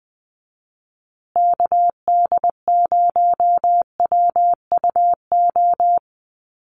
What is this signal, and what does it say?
Morse code, heard as a keyed tone, about 20 wpm, 700 Hz. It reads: KD0WUO
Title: USA Call Signs Repeated 3x 20wpm